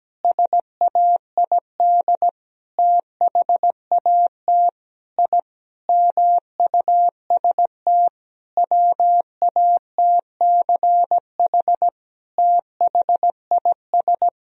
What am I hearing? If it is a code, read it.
Morse code, heard as a keyed tone, about 17 wpm, 700 Hz. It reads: SAID THAT I MUST WATCH THIS
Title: Sentences from Top 1000 Words - Review Part 4 - 17wpm